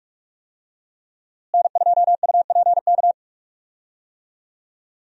Morse code, heard as a keyed tone, about 46 wpm, 700 Hz. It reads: N2UPK
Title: USA Call Signs - 5 Characters - Repeated 3x - 45wpm